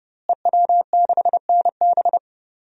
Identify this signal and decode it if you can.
Morse code, heard as a keyed tone, about 30 wpm, 700 Hz. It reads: EW6NB
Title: DXCC-100 Call Signs - 5 Characters - Repeated 3x - 30wpm